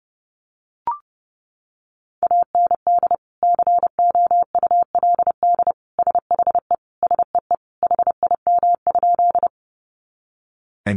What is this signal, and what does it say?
Morse code, heard as a keyed tone, about 30 wpm, 700 Hz. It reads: AND COULD SHE SEE HIM?
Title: Sentences from Top 100 Words 30wpm